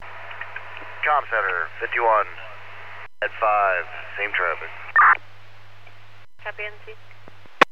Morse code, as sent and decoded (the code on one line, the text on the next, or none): none